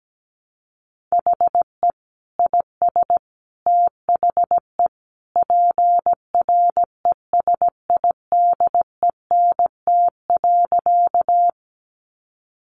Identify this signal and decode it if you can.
Morse code, heard as a keyed tone, about 17 wpm, 700 Hz. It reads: HE IS THE PRESIDENT.
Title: Sentences from Top 1000 Words 17wpm